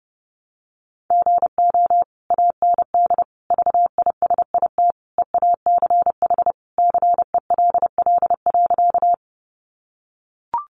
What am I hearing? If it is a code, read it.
Morse code, heard as a keyed tone, about 30 wpm, 700 Hz. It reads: GO AND VISIT EACH CELL.
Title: Sentences from Top 1100 Words 30wpm